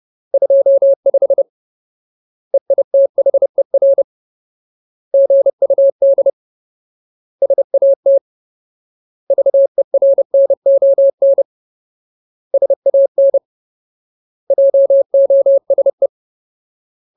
Morse code, heard as a keyed tone, about 30 wpm, 550 Hz. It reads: 25 EITHER GUD SAT VERNON SAN JOSE
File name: Ultimate Word List / 30WPM Ultimate